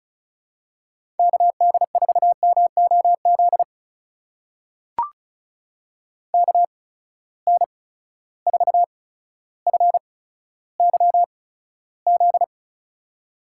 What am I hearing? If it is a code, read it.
Morse code, heard as a keyed tone, about 36 wpm, spaced as for 15 wpm, 700 Hz. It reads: KD4MOZ KN4FYZ
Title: USA Call Signs - 6 Characters - 8x Character Spacing - 35wpm